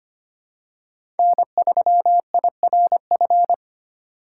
Morse code, heard as a keyed tone, about 25 wpm, 700 Hz. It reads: N3IRF
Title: USA Call Signs - 5 Characters - 2x Character Spacing - 25wpm